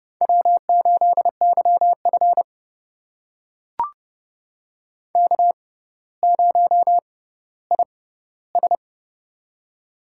Morse code, heard as a keyed tone, about 30 wpm, spaced as for 15 wpm, 700 Hz. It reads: W8YF K0IS
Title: USA Call Signs - 4 Characters - 6x Character Spacing - 30wpm